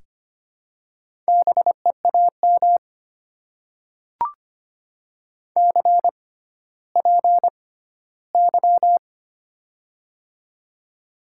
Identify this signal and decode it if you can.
Morse code, heard as a keyed tone, about 25 wpm, spaced as for 14 wpm, 700 Hz. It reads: BEAM CPY
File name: Top 90 QSO Elements - 6x Character Spacing - 25wpm